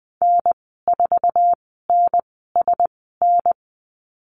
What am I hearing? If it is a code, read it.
Morse code, heard as a keyed tone, about 20 wpm, spaced as for 15 wpm, 700 Hz. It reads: N4NSN
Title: 161 Lesson 26g - Letter g - Call Signs 20wpm